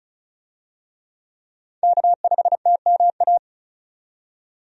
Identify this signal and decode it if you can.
Morse code, heard as a keyed tone, about 36 wpm, 700 Hz. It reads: K5TMA
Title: USA Call Signs - 5 Characters - 35wpm